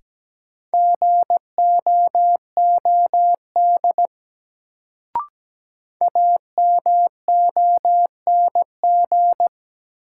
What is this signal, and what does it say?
Morse code, heard as a keyed tone, about 17 wpm, 700 Hz. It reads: GOOD AMONG
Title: Sets of 2 Words - New Word Drill - 17wpm